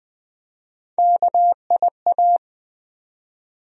Morse code, heard as a keyed tone, about 20 wpm, 700 Hz. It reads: KIA
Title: USA Call Sign Postfixes - 3 Letters 20wpm